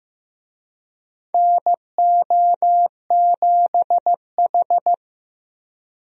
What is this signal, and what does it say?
Morse code, heard as a keyed tone, about 15 wpm, 700 Hz. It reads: NO7H
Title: USA Call Signs - 4 Characters - 15wpm